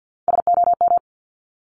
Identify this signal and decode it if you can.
Morse code, heard as a keyed tone, about 50 wpm, 690 Hz. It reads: SOM
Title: Top Three Letter Combinations in Words 50wpm